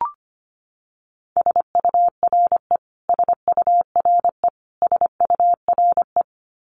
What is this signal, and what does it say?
Morse code, heard as a keyed tone, about 25 wpm, 700 Hz. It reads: SURE SURE SURE
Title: Top 300 Words Repeated 25wpm